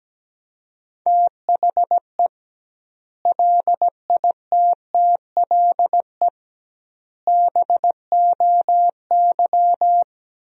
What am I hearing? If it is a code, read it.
Morse code, heard as a keyed tone, about 17 wpm, 700 Hz. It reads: THE LITTLE BOY
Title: 2x Word-Spacing - Sentences from Top 500 Words - 17wpm